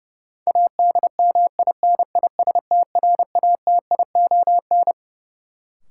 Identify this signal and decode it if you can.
Morse code, heard as a keyed tone, about 32 wpm, 700 Hz. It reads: ADMINISTRATION
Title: Sets of 5 Words - New Word Drill - 30wpm